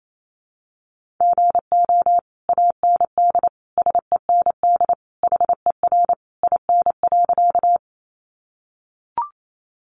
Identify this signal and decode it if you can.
Morse code, heard as a keyed tone, about 28 wpm, 700 Hz. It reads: GO AND SEND HER IN.
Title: Sentences from Top 1500 Words 28wpm